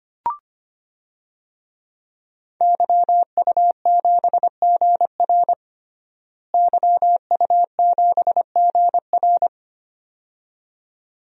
Morse code, heard as a keyed tone, about 25 wpm, 700 Hz. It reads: YU7GR YU7GR
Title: DXCC-100 Call Signs - 5 Characters - Repeated 2x - 25wpm